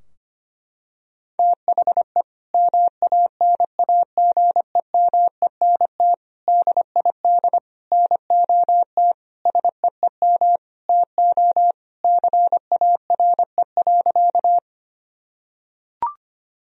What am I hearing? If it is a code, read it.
Morse code, heard as a keyed tone, about 25 wpm, 700 Hz. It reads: THE MANAGEMENT DID NOT SEEM TO CARE.
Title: Sentences from Top 1000 Words - Review Part 4 - 25wpm